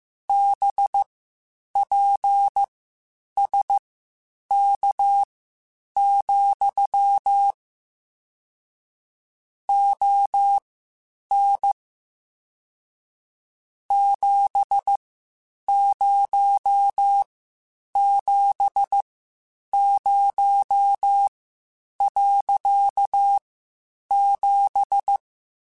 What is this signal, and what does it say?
Morse code, heard as a keyed tone, about 15 wpm, spaced as for 8 wpm, 800 Hz. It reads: BPSK, ON 7070.7